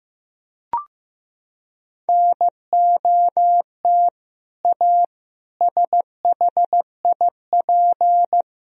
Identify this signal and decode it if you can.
Morse code, heard as a keyed tone, about 15 wpm, 700 Hz. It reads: NOT A SHIP
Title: Sentences from Top 1000 Words - Review Part 2 - 15wpm